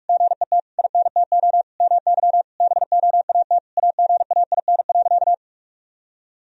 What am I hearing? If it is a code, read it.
Morse code, heard as a keyed tone, about 48 wpm, 700 Hz. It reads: GET INTO MY BOAT AGAIN.
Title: Sentences from Top 1000 Words - Review Part 3 - 45wpm